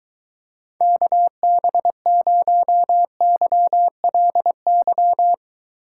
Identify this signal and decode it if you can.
Morse code, heard as a keyed tone, about 23 wpm, 700 Hz. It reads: KB0YLY